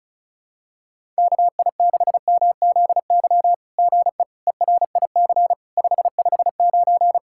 Morse code, heard as a keyed tone, about 34 wpm, 700 Hz. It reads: KI6MZY GE ERIC 559
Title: SOTA QSO Practice 35wpm